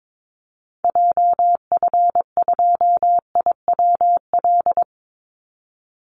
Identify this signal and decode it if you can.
Morse code, heard as a keyed tone, about 22 wpm, 700 Hz. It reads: JF2IWL
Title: Call Signs Repeated 2x - 22wpm